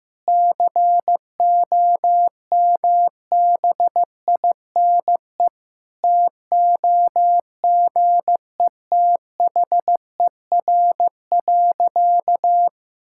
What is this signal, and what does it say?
Morse code, heard as a keyed tone, about 16 wpm, 700 Hz. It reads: COMBINE TOGETHER.
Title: Sentences from Top 1600 Words 15wpm